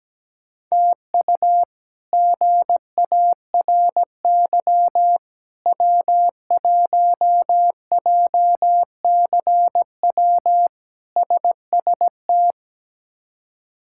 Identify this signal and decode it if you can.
Morse code, heard as a keyed tone, about 17 wpm, 700 Hz. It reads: TU GARY W1JCW SST
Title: K1USN SST QSO Practice - 17wpm